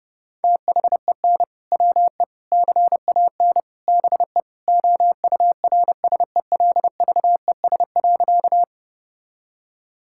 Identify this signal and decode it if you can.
Morse code, heard as a keyed tone, about 30 wpm, 700 Hz. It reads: THEN WE CAN BE OURSELVES.
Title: Sentences from Top 1600 Words 30wpm